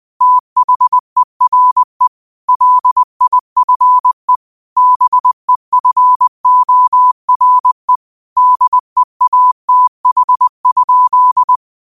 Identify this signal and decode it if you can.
Morse code, heard as a keyed tone, about 20 wpm, 1000 Hz. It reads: THERE LIFE BEFORE DEATH?
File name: qotdfiles / 20wpm